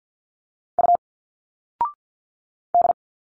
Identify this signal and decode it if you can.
Morse code, heard as a keyed tone, about 52 wpm, 700 Hz. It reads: U D
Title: Single Letters 50wpm